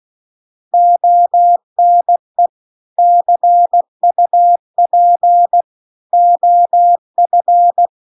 Morse code, heard as a keyed tone, about 16 wpm, 700 Hz. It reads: ONE CUP OF